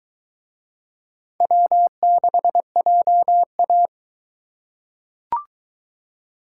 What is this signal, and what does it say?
Morse code, heard as a keyed tone, about 22 wpm, 700 Hz. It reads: W6JA